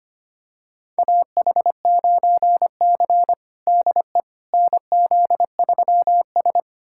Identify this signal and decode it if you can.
Morse code, heard as a keyed tone, about 25 wpm, 700 Hz. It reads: AH9C DE NZ3S
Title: USA Callsign DE Callsign - 4 Characters - 25wpm